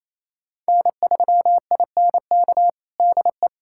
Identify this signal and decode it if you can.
Morse code, heard as a keyed tone, about 28 wpm, 700 Hz. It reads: N3INK DE
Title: USA Callsign DE Callsign - 5 Characters - 28wpm